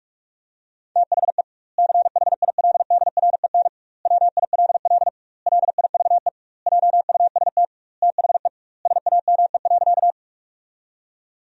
Morse code, heard as a keyed tone, about 42 wpm, 700 Hz. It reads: THE CHILDREN WILL LIVE JUST THE SAME.
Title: 2x Word-Spacing - Sentences from Top 300 Words - 45wpm